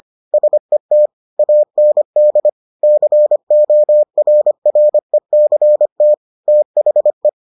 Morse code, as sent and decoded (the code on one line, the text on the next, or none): ... - / .- -. -.. / -.-. --- .-. .-. -.-. - / - .... .
SET AND CORRECT THE